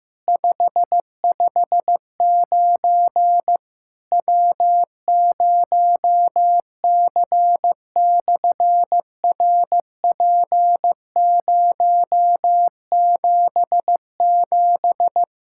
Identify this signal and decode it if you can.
Morse code, heard as a keyed tone, about 15 wpm, 700 Hz. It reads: 559 W0C/RP077
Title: SOTA QSO Practice 15wpm